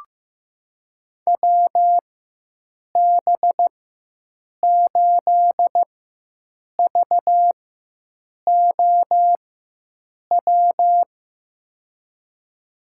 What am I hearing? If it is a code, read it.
Morse code, heard as a keyed tone, about 15 wpm, spaced as for 7 wpm, 700 Hz. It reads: WB8VOW
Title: USA Call Signs - 6 Characters - 4x Character Spacing - 15wpm